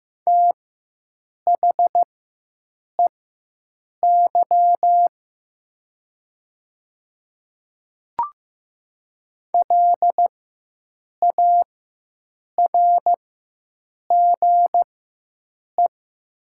Morse code, heard as a keyed tone, about 15 wpm, spaced as for 7 wpm, 700 Hz. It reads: THEY LARGE
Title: 4x Character Spacing - Top 200 Words - 15wpm